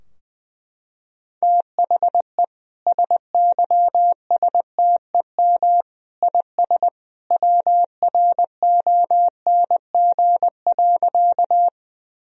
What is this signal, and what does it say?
Morse code, heard as a keyed tone, about 20 wpm, 700 Hz. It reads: THE SYSTEM IS WRONG.
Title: Sentences from Top 1000 Words - Review Part 3 - 20wpm